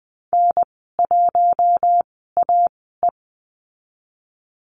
Morse code, heard as a keyed tone, about 20 wpm, spaced as for 15 wpm, 700 Hz. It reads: N1AE
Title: 042 Lesson 8g - Number 1 - Call Signs 20wpm